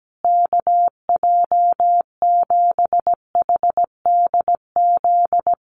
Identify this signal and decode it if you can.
Morse code, heard as a keyed tone, about 17 wpm, 700 Hz. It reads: KJ7HDZ
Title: USA Call Signs Repeated 3x 17wpm